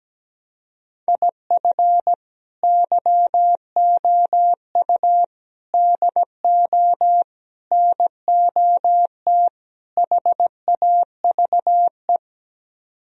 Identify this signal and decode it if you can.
Morse code, heard as a keyed tone, about 17 wpm, 700 Hz. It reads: IF YOU DO NOT HAVE
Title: Speed-Racing - Sets of 5 Words 17wpm